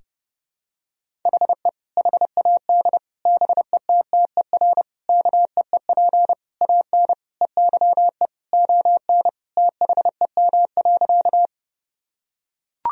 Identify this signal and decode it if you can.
Morse code, heard as a keyed tone, about 30 wpm, 700 Hz. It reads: HE HAD BETTER KEEP AN EYE ON THEM.
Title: Sentences from Top 1000 Words - Review Part 1 - 30wpm